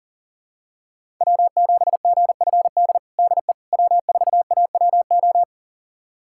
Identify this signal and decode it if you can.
Morse code, heard as a keyed tone, about 40 wpm, 700 Hz. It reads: W7GFD DE W4AWO
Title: USA Callsign DE Callsign - 5 Characters - 40wpm